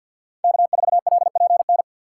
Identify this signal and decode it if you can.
Morse code, heard as a keyed tone, about 53 wpm, 700 Hz. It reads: K4LPN
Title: USA Call Signs - 5 Characters - 6x Character Spacing - 50wpm